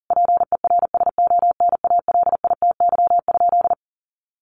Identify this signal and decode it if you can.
Morse code, heard as a keyed tone, about 40 wpm, 700 Hz. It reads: PERSONALITY?
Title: Sentences from Top 1400 Words 40wpm